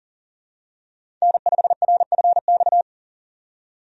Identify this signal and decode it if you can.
Morse code, heard as a keyed tone, about 40 wpm, 700 Hz. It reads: N5RFX
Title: USA Call Signs - 5 Characters - 40wpm